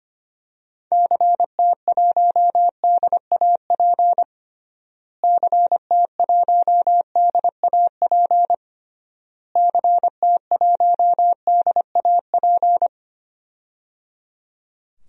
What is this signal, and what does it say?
Morse code, heard as a keyed tone, about 25 wpm, 700 Hz. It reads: CT1DAP CT1DAP CT1DAP
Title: DXCC-100 Call Signs - 6 Characters - Repeated 3x - 25wpm